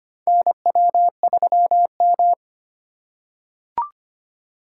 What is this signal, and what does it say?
Morse code, heard as a keyed tone, about 25 wpm, 700 Hz. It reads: NW3M
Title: USA Call Signs - 4x Character Spacing - 25wpm